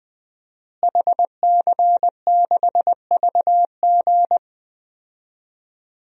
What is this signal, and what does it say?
Morse code, heard as a keyed tone, about 20 wpm, 700 Hz. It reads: HC6VG
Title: DXCC-100 Call Signs - 5 Characters - Repeated 3x - 20wpm